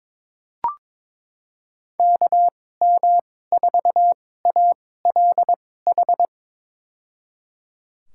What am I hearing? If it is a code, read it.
Morse code, heard as a keyed tone, about 22 wpm, spaced as for 16 wpm, 700 Hz. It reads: KM4ALH
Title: USA Call Signs - 6 Characters - 2x Character Spacing - 22wpm